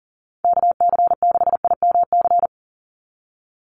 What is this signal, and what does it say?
Morse code, heard as a keyed tone, about 40 wpm, 700 Hz. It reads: KC6IMC
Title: USA Call Signs 40wpm